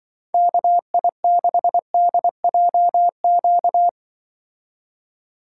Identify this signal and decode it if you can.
Morse code, heard as a keyed tone, about 24 wpm, 700 Hz. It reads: KI6DJQ
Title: Speed Racing - USA 2x3 Call Signs 20wpm